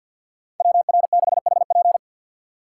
Morse code, heard as a keyed tone, about 51 wpm, 700 Hz. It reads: WR6HP
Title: USA Call Signs - 2x Character Spacing - 50wpm